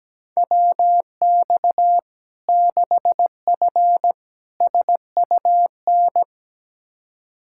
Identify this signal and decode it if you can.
Morse code, heard as a keyed tone, about 17 wpm, 700 Hz. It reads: WX 6F SUN
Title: Practice QSO - Exchange 2 - Info - 17wpm